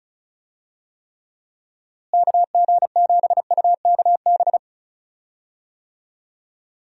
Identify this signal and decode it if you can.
Morse code, heard as a keyed tone, about 36 wpm, 700 Hz. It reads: KG7UKB
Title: USA Call Signs - 6 Characters - Repeated 3x - 35wpm